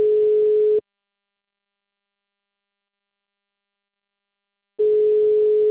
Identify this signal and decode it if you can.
Ringback tone (Europe, 425 Hz).
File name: out- -2004- - - .0